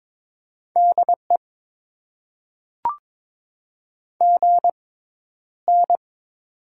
Morse code, heard as a keyed tone, about 22 wpm, spaced as for 11 wpm, 700 Hz. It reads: DE GN
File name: Top 90 QSO Elements - 6x Character Spacing - 22wpm